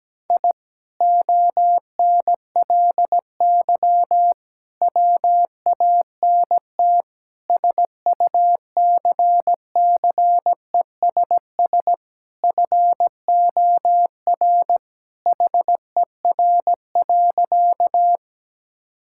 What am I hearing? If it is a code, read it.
Morse code, heard as a keyed tone, about 17 wpm, 700 Hz. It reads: I ONLY WANT SUCCESS FOR HER.
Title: Sentences from Top 1000 Words - Review Part 6 - 17wpm